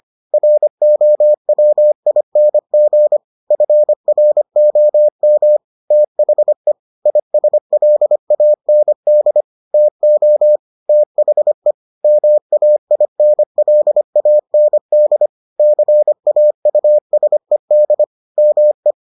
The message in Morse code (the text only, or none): ROWING FROM THE ISLAND TO THE MAINLAND CAUSED ME